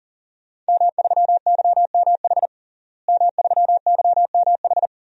Morse code, heard as a keyed tone, about 40 wpm, 690 Hz. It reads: M3YMH M3YMH
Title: DXCC-100 Call Signs - 5 Characters - Repeated 2x - 40wpm